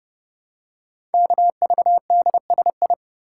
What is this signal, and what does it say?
Morse code, heard as a keyed tone, about 28 wpm, 700 Hz. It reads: KVDSI
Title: Code Groups - 5 Letters - 30wpm